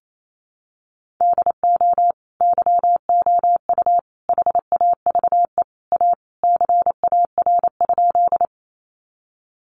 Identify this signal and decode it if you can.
Morse code, heard as a keyed tone, about 28 wpm, 700 Hz. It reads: DO YOU HAVE A CAR?
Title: Sentences from Top 300 Words 28wpm